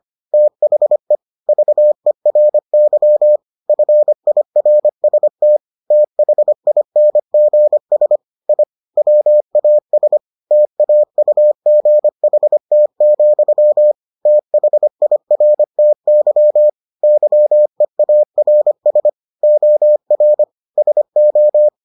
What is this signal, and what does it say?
Morse code, heard as a keyed tone, about 25 wpm, 600 Hz. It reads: THE VERY FIRST THINGS I WAS TAUGHT, THIRTY YEARS OR SO